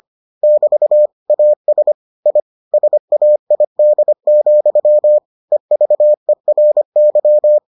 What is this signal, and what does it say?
Morse code, heard as a keyed tone, about 25 wpm, 600 Hz. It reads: <BT> AS I SAID, EVERY